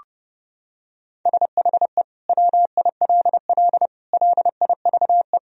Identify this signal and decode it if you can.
Morse code, heard as a keyed tone, about 30 wpm, 700 Hz. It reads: SHE WILL LIVE
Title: Sentences from Top 1000 Words - Review Part 4 - 30wpm